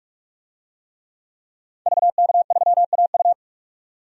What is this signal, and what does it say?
Morse code, heard as a keyed tone, about 45 wpm, 700 Hz. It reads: VK3AU
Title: DXCC-100 Call Signs - 5 Characters - 45wpm